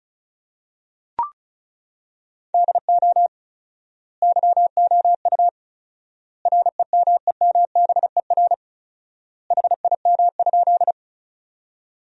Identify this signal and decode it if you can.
Morse code, heard as a keyed tone, about 36 wpm, 700 Hz. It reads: DO YOU REMEMBER HIM?
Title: 4x Word-Spacing - Sentences from Top 400 Words - 35wpm